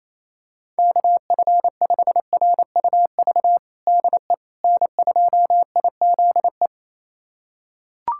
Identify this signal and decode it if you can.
Morse code, heard as a keyed tone, about 28 wpm, 700 Hz. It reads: KF5RUV DE N2IZE
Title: USA Callsign DE Callsign 28wpm